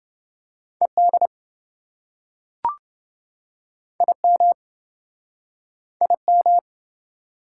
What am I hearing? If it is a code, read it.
Morse code, heard as a keyed tone, about 29 wpm, 700 Hz. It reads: ED IM IM